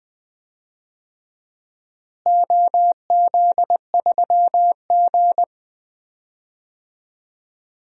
Morse code, heard as a keyed tone, about 20 wpm, 700 Hz. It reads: OZ3G
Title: DXCC-100 Call Signs - 4 Characters - Repeated 2x - 20wpm